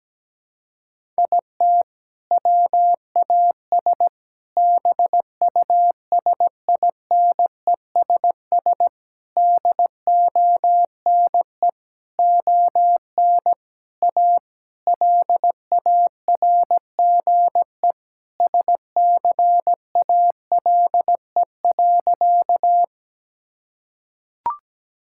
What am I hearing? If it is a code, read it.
Morse code, heard as a keyed tone, about 17 wpm, 700 Hz. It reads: IT WAS BUSINESS DONE ON A LARGE SCALE.
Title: Sentences from Top 1000 Words - Review Part 5 - 17wpm